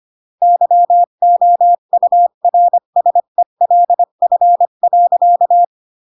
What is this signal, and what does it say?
Morse code, heard as a keyed tone, about 25 wpm, 700 Hz. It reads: YOURSELF.
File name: CW / jodorowsky / 82 commandments 25-20